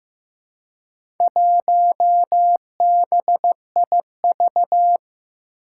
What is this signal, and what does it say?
Morse code, heard as a keyed tone, about 15 wpm, 700 Hz. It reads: 1BIV
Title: Four Challenging Letter-Number Combinations 15wpm